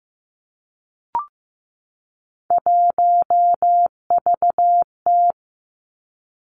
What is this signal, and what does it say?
Morse code, heard as a keyed tone, about 15 wpm, 700 Hz. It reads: 1VT
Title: Three Challenging Letter-Number Combinations 15wpm